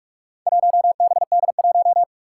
Morse code, heard as a keyed tone, about 44 wpm, 700 Hz. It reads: 1BD1
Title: Four Challenging Letter-Number Combinations 45wpm